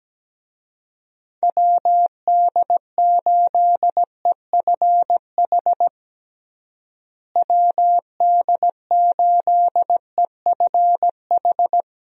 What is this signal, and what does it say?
Morse code, heard as a keyed tone, about 17 wpm, 700 Hz. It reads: WD8EFH WD8EFH
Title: USA Call Signs - 6 Characters - Repeated 3x - 17wpm